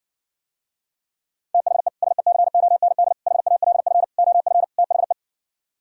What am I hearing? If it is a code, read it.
Morse code, heard as a keyed tone, about 69 wpm, 700 Hz. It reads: THE SECOND HALF OF THE